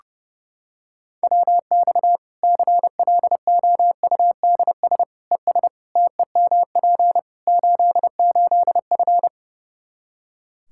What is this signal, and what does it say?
Morse code, heard as a keyed tone, about 30 wpm, 700 Hz. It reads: WX CLOUDS ES TEMP 88F
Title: K1USN SST QSO Practice 30wpm